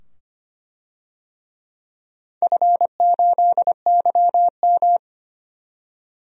Morse code, heard as a keyed tone, about 25 wpm, 700 Hz. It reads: F8YM
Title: DXCC-100 Call Signs - 4 Characters - 25wpm